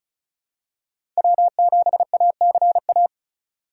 Morse code, heard as a keyed tone, about 34 wpm, 700 Hz. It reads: W7ACA
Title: USA Call Signs - 5 Characters - 2x Character Spacing - 35wpm